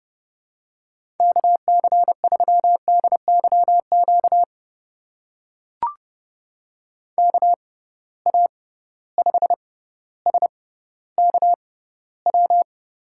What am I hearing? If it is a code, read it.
Morse code, heard as a keyed tone, about 31 wpm, spaced as for 15 wpm, 700 Hz. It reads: KC3DYQ KA5SKW
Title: USA Call Signs - 6 Characters - 6x Character Spacing - 30wpm